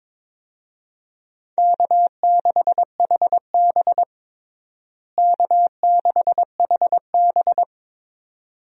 Morse code, heard as a keyed tone, about 21 wpm, 700 Hz. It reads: K6HB K6HB
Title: USA Call Signs - 4 Characters - Repeated 2x - 22wpm